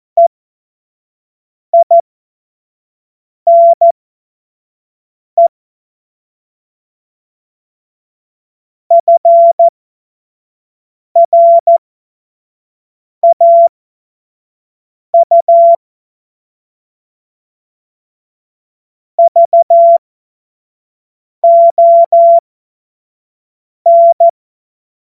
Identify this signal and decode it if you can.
Morse code, heard as a keyed tone, about 14 wpm, spaced as for 5 wpm, 680 Hz. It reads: EINE FRAU VON
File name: Morsekurs / Witze / Witz 13 Februar 2024 25 F70